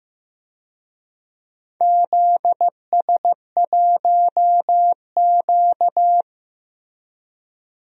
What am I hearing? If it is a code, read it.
Morse code, heard as a keyed tone, about 15 wpm, 700 Hz. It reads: ZS1Q
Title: DXCC-100 Call Signs - 4 Characters - Repeated 3x - 15wpm